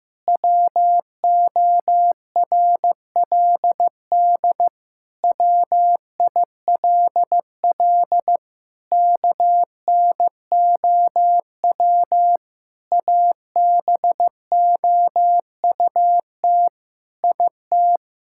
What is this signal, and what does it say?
Morse code, heard as a keyed tone, about 14 wpm, 700 Hz. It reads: WORLD WILL KNOW ABOUT IT
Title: Sentences from Top 1000 Words - Review Part 5 - 15wpm